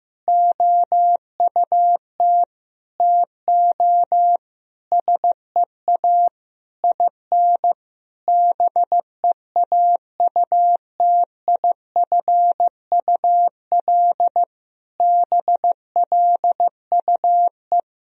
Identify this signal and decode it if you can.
Morse code, heard as a keyed tone, about 16 wpm, 700 Hz. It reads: OUT TO SEA IN BEAUTIFUL BLUE
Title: Sentences from Top 1000 Words - Review Part 3 - 15wpm